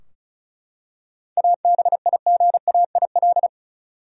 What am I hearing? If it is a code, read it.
Morse code, heard as a keyed tone, about 36 wpm, 700 Hz. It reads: ABIGAIL
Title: USA Common Names - 35wpm